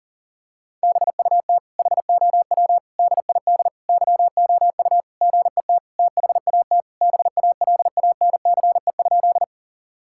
Morse code, heard as a keyed tone, about 40 wpm, 700 Hz. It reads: BUT HOW DID YOU GET THAT BALANCE?
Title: Sentences from Top 1000 Words - Review Part 6 - 40wpm